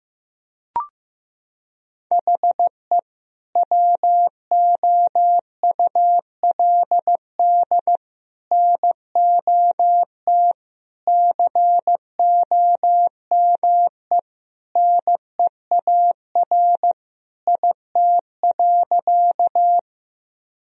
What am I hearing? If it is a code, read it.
Morse code, heard as a keyed tone, about 15 wpm, 700 Hz. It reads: HE WOULD NOT COME NEAR IT.
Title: Sentences from Top 1000 Words - Review Part 1 - 15wpm